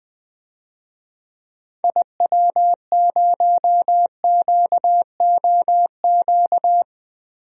Morse code, heard as a keyed tone, about 20 wpm, 700 Hz. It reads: IW0QOQ
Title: DXCC-100 Call Signs - 6 Characters - 20wpm